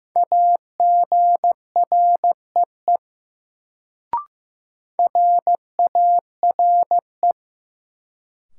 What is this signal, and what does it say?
Morse code, heard as a keyed tone, about 15 wpm, 700 Hz. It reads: AGREE RARE
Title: Sentences from Top 1100 Words - New Word Drill Alt - 15wpm